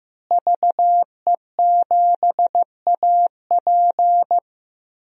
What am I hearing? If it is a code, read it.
Morse code, heard as a keyed tone, about 16 wpm, 700 Hz. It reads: VE7AP
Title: DXCC-100 Call Signs - 5 Characters - Repeated 3x - 15wpm